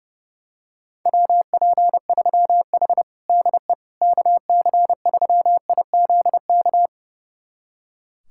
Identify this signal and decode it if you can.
Morse code, heard as a keyed tone, about 30 wpm, 700 Hz. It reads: WP3H DE KC3IZK
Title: USA Callsign DE Callsign 30wpm